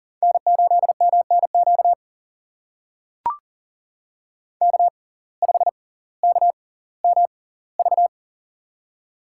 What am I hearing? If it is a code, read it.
Morse code, heard as a keyed tone, about 40 wpm, spaced as for 20 wpm, 700 Hz. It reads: N8MNQ K5KMV
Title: USA Call Signs - 5 Characters - 6x Character Spacing - 40wpm